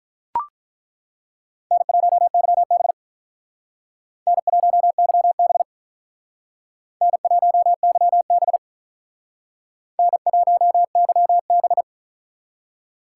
Morse code, heard as a keyed tone, about 48 wpm, 700 Hz. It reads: N1YB N1YB N1YB N1YB